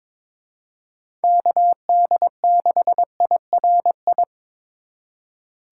Morse code, heard as a keyed tone, about 22 wpm, 700 Hz. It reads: KD6IRI
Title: K1USN SST QSO Practice - 22wpm